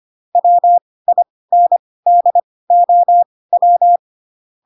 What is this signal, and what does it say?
Morse code, heard as a keyed tone, about 25 wpm, spaced as for 18 wpm, 700 Hz. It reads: WINDOW
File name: Pinocchio / Pinocchio ch6 12 25 ik1hge